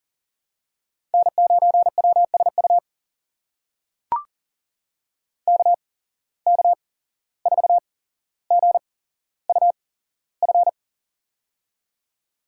Morse code, heard as a keyed tone, about 40 wpm, spaced as for 16 wpm, 700 Hz. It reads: N9WSU KK4GUF
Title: USA Call Signs - 8x Character Spacing - 40wpm